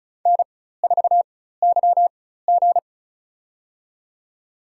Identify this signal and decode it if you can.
Morse code, heard as a keyed tone, about 36 wpm, spaced as for 16 wpm, 700 Hz. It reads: N4YG
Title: USA Call Signs - 4 Characters - 4x Character Spacing - 35wpm